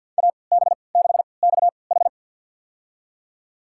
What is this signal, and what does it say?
Morse code, heard as a keyed tone, about 50 wpm, spaced as for 28 wpm, 700 Hz. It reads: AB6XH
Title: USA Call Signs - 3x Character Spacing - 50wpm